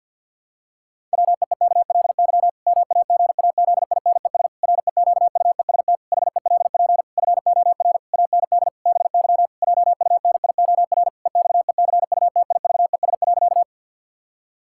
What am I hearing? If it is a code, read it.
Morse code, heard as a keyed tone, about 50 wpm, 700 Hz. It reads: WEEKLY MAGAZINES REQUEST HELP FOR AND BY JUNIOR EXECUTIVES.
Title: Pangram Challenge v2 - 50wpm